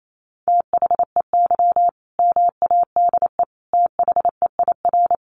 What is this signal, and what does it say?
Morse code, heard as a keyed tone, about 28 wpm, 700 Hz. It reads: THEY MADE THEIR
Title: Sentences from Top 1400 Words 28wpm